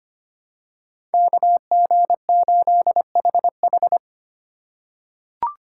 Morse code, heard as a keyed tone, about 25 wpm, 700 Hz. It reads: KG8HH
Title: USA Call Signs - 5 Characters - 2x Character Spacing - 25wpm